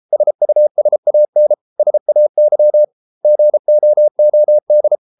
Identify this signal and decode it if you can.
Morse code, heard as a keyed tone, about 33 wpm, 600 Hz. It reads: SUSAN SAY GOOD